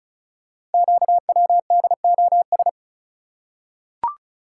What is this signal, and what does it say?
Morse code, heard as a keyed tone, about 36 wpm, 690 Hz. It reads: QWDOS
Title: Code Groups - 5 Letters - 35wpm